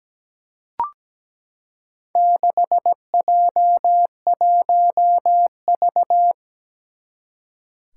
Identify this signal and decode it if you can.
Morse code, heard as a keyed tone, about 17 wpm, 700 Hz. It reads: 6J1V
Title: Four Challenging Letter-Number Combinations 17wpm